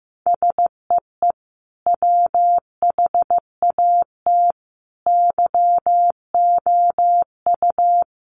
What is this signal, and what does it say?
Morse code, heard as a keyed tone, about 15 wpm, 700 Hz. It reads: SEE WHAT YOU
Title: Sentences from Top 500 Words 15wpm